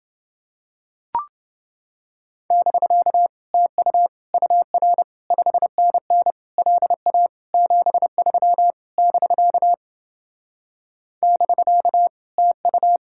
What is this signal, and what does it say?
Morse code, heard as a keyed tone, about 28 wpm, 700 Hz. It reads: <BK> TU UR 5NN LA 73 <BK> <BK> TU